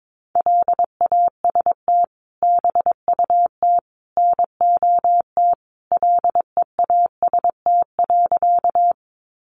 Morse code, heard as a keyed tone, about 22 wpm, 700 Hz. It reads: LAST BUT NOT LEAST.
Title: Sentences from Top 700 Words 22wpm